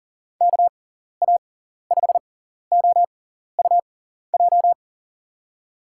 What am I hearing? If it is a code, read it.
Morse code, heard as a keyed tone, about 40 wpm, spaced as for 14 wpm, 700 Hz. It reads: KA5OUJ
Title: USA Call Signs - 6 Characters - 6x Character Spacing - 40wpm